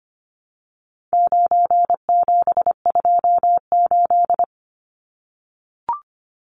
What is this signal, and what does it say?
Morse code, heard as a keyed tone, about 25 wpm, 700 Hz. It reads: 9728
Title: Numbers - 4-Digit 25wpm